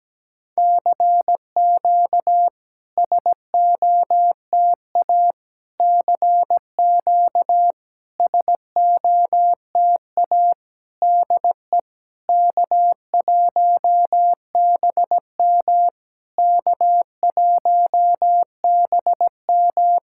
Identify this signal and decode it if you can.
Morse code, heard as a keyed tone, about 17 wpm, 700 Hz. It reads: CQ SOTA CQ SOTA DE K1BM K1BM
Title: SOTA QSO Practice 17wpm